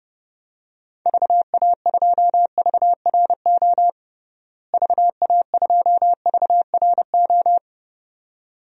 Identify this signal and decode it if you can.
Morse code, heard as a keyed tone, about 29 wpm, 700 Hz. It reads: VA2VRO VA2VRO
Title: DXCC-100 Call Signs - 6 Characters - Repeated 2x - 30wpm